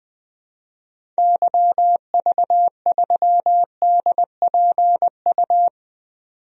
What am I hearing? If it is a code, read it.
Morse code, heard as a keyed tone, about 20 wpm, 700 Hz. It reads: YV3DPU
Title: DXCC-100 Call Signs - 6 Characters - 20wpm